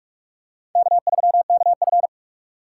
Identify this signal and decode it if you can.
Morse code, heard as a keyed tone, about 46 wpm, 700 Hz. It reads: K3KF
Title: USA Call Signs - 2x Character Spacing - 45wpm